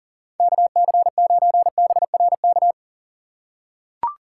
Morse code, heard as a keyed tone, about 38 wpm, 700 Hz. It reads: KC9BRK
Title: USA Call Signs - 6x Character Spacing - 40wpm